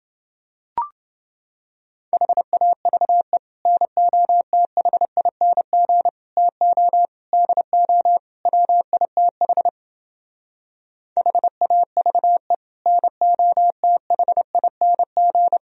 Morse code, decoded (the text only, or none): HAVE NOTHING TO DO WITH HAVE NOTHING